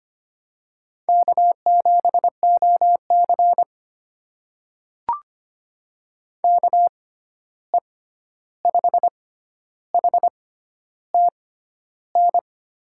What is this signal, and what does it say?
Morse code, heard as a keyed tone, about 25 wpm, spaced as for 11 wpm, 700 Hz. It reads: K7OC KE5HTN
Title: USA Call Signs - 6x Character Spacing - 25wpm